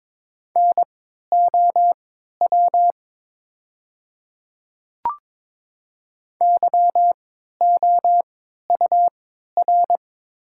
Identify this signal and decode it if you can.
Morse code, heard as a keyed tone, about 22 wpm, spaced as for 13 wpm, 700 Hz. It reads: NOW YOUR
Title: 3x Character Spacing - Top 100 Words - 22wpm